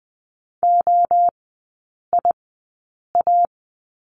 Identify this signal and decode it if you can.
Morse code, heard as a keyed tone, about 20 wpm, spaced as for 8 wpm, 700 Hz. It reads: OIA
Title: 121 Lesson 20e - Number 3 - Three Characters 20wpm